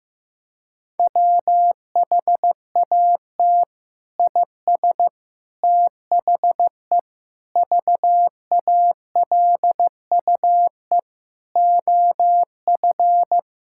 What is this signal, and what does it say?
Morse code, heard as a keyed tone, about 14 wpm, 700 Hz. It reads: WHAT IS THE VALUE OF
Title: Sentences from Top 1000 Words - Review Part 6 - 15wpm